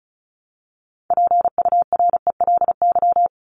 Morse code, heard as a keyed tone, about 34 wpm, 700 Hz. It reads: PURELY
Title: LY Suffix - Top 100 Words with Base Words 35wpm